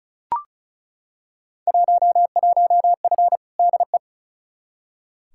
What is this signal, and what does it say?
Morse code, heard as a keyed tone, about 36 wpm, 700 Hz. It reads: 11F DE
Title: Field Day Exchanges Encore - 35wpm